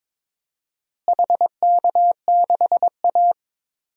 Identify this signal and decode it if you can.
Morse code, heard as a keyed tone, about 22 wpm, 700 Hz. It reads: HK6A
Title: DXCC-100 Call Signs - 4 Characters - Repeated 2x - 22wpm